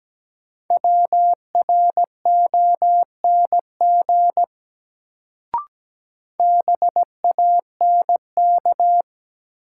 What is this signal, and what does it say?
Morse code, heard as a keyed tone, about 17 wpm, 700 Hz. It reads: WRONG BANK
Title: Sentences from Top 900 Words - New Word Drill - 17wpm